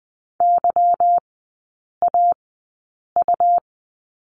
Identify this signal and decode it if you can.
Morse code, heard as a keyed tone, about 20 wpm, 700 Hz. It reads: Y A U
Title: 145 Lesson 24e - Letter y - Three Characters 20wpm